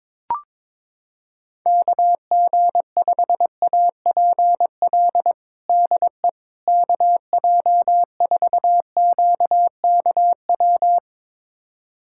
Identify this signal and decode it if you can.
Morse code, heard as a keyed tone, about 22 wpm, 700 Hz. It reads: KG5APL DE KJ4QKW
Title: USA Callsign DE Callsign 22wpm